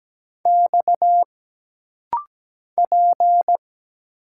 Morse code, heard as a keyed tone, about 17 wpm, 700 Hz. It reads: X P
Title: Single Letters - Rapid-Fire 17wpm